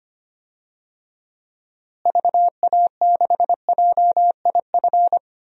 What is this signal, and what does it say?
Morse code, heard as a keyed tone, about 25 wpm, 700 Hz. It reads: VA6JIF
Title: DXCC-100 Call Signs - 6 Characters - Repeated 3x - 25wpm